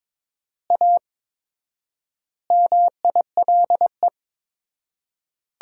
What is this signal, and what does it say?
Morse code, heard as a keyed tone, about 22 wpm, 700 Hz. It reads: A MILE
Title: 4x Word-Spacing - Sentences from Top 500 Words - 22wpm